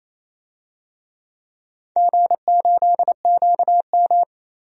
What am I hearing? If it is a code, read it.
Morse code, heard as a keyed tone, about 28 wpm, 700 Hz. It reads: G8QM
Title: DXCC-100 Call Signs - 4 Characters - 28wpm